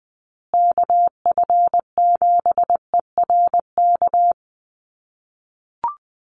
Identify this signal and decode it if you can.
Morse code, heard as a keyed tone, about 20 wpm, 700 Hz. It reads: KF7ERK
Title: USA Call Signs Repeated 3x 20wpm